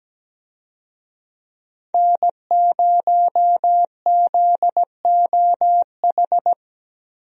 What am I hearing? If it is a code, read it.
Morse code, heard as a keyed tone, about 17 wpm, 700 Hz. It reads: N0ZOH
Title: USA Call Signs - 5 Characters - Repeated 2x - 17wpm